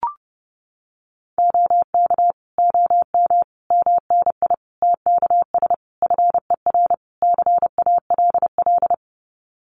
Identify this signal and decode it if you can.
Morse code, heard as a keyed tone, about 30 wpm, 700 Hz. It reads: OK OM MNI TKS FER CALL
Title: Common QSO Phrases 30wpm